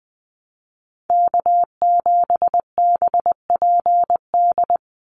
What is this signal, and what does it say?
Morse code, heard as a keyed tone, about 20 wpm, 700 Hz. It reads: K7BPD
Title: USA Call Signs 20wpm